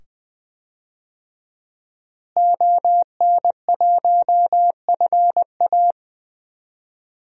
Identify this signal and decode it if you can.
Morse code, heard as a keyed tone, about 20 wpm, 700 Hz. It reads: ON1FA
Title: DXCC-100 Call Signs - 5 Characters - Repeated 2x - 20wpm